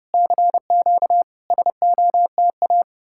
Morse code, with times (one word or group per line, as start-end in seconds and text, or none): 0.1-1.2 CQ
1.5-2.8 SOTA